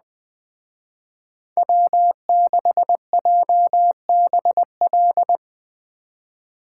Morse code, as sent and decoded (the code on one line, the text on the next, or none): .-- -.... .--- -... .-..
W6JBL